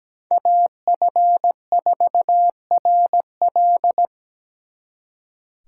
Morse code, heard as a keyed tone, about 20 wpm, 700 Hz. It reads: AF4RL
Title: Speed Racing - USA 2x2 Call Signs 17wpm